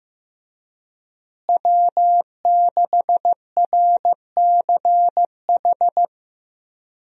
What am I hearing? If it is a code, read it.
Morse code, heard as a keyed tone, about 15 wpm, 700 Hz. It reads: W6RCH
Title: USA Call Signs - 5 Characters - Repeated 2x - 15wpm